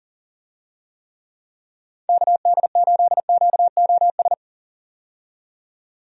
Morse code, heard as a keyed tone, about 40 wpm, 700 Hz. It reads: KD8QOS
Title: USA Call Signs - 6 Characters - 40wpm